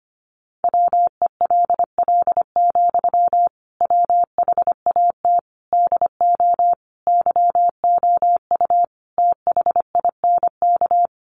Morse code, heard as a keyed tone, about 25 wpm, 700 Hz. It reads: WELL, WHAT DO YOU THINK
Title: Sentences from Top 200 Words 25wpm